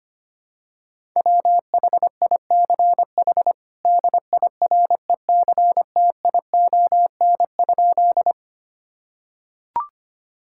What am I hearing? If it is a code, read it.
Morse code, heard as a keyed tone, about 25 wpm, 700 Hz. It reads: WHICH DIRECTION?
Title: Sentences from Top 1000 Words - Review Part 5 - 25wpm